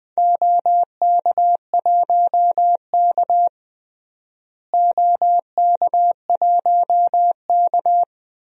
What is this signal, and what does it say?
Morse code, heard as a keyed tone, about 21 wpm, 700 Hz. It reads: OK1K OK1K
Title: DXCC-100 Call Signs - 4 Characters - Repeated 3x - 20wpm